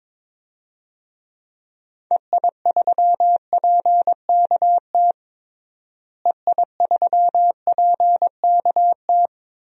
Morse code, heard as a keyed tone, about 21 wpm, 700 Hz. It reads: EI3PKT EI3PKT
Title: DXCC-100 Call Signs - 6 Characters - Repeated 2x - 22wpm